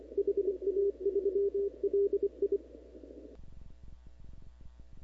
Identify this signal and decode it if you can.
Morse code, heard as a keyed tone, about 25 wpm, 400 Hz. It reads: HA3LI
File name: CW / 2005-k5tr arrl cw dx -15:00